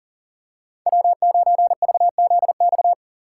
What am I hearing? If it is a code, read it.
Morse code, heard as a keyed tone, about 37 wpm, 700 Hz. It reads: W9VZX
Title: USA Call Signs - 5 Characters - 6x Character Spacing - 40wpm